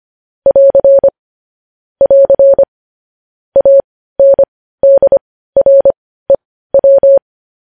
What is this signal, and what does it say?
Morse code, heard as a keyed tone, about 25 wpm, spaced as for 15 wpm, 550 Hz. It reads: <AR> <AR> ANDREW
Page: news headlines in morse code at 15 wpm